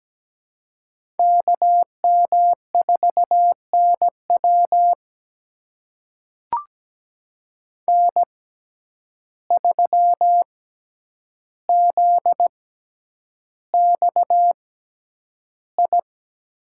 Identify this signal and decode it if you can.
Morse code, heard as a keyed tone, about 17 wpm, spaced as for 9 wpm, 700 Hz. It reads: KM4NW N3ZXI
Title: USA Call Signs - 5 Characters - 6x Character Spacing - 17wpm